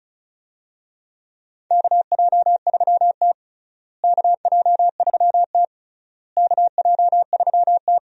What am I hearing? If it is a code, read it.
Morse code, heard as a keyed tone, about 34 wpm, 700 Hz. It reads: KJ3T KJ3T KJ3T
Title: USA Call Signs - 4 Characters - Repeated 3x - 35wpm